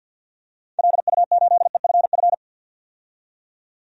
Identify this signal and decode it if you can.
Morse code, heard as a keyed tone, about 53 wpm, 700 Hz. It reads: LU8EFF
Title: DXCC-100 Call Signs - 6 Characters - 50wpm